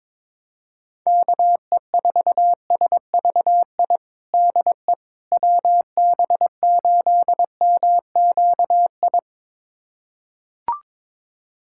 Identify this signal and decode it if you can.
Morse code, heard as a keyed tone, about 22 wpm, 700 Hz. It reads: KE4SVI DE WB8MQI
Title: USA Callsign DE Callsign - 6 Characters - 22wpm